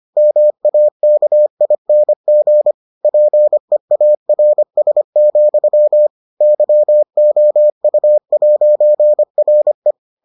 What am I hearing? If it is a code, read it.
Morse code, heard as a keyed tone, about 25 wpm, 600 Hz. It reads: MAKING PEARS, YOU'RE